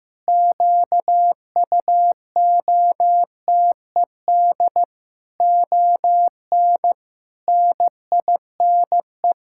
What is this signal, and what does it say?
Morse code, heard as a keyed tone, about 16 wpm, 700 Hz. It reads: QUOTED ON NINE
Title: Pangram Challenge v2 - 15wpm